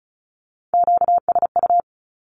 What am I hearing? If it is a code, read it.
Morse code, heard as a keyed tone, about 34 wpm, 700 Hz. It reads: QSU
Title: Q-Codes Repeated 3x 35wpm